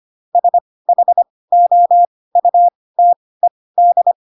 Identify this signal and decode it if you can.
Morse code, heard as a keyed tone, about 25 wpm, spaced as for 18 wpm, 700 Hz. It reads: SHOUTED
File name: Pinocchio / Pinocchio ch2 12 25 ik1hge